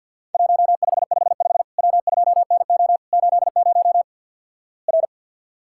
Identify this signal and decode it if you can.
Morse code, heard as a keyed tone, about 51 wpm, 700 Hz. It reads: 1555 W2NO 80 R
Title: Worked All Europe DX Contest - QTC Exchange - 50wpm